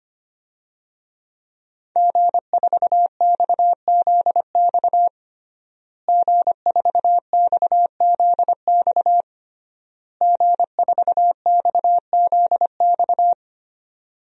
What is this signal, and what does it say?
Morse code, heard as a keyed tone, about 25 wpm, 700 Hz. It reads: G4XZX G4XZX G4XZX
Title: DXCC-100 Call Signs - 5 Characters - Repeated 3x - 25wpm